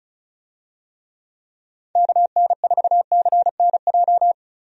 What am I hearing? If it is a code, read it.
Morse code, heard as a keyed tone, about 34 wpm, 700 Hz. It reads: KN4CNJ
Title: USA Call Signs - 6 Characters - 35wpm